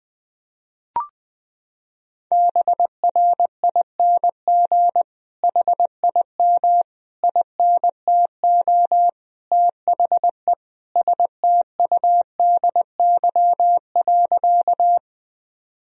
Morse code, heard as a keyed tone, about 20 wpm, 700 Hz. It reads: BRING HIM INTO THE STUDY.
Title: Sentences from Top 1000 Words - Review Part 1 - 20wpm